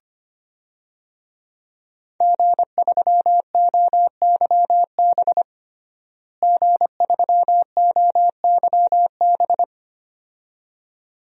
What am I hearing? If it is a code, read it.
Morse code, heard as a keyed tone, about 25 wpm, 700 Hz. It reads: G3OYB G3OYB
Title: DXCC-100 Call Signs - 5 Characters - Repeated 2x - 25wpm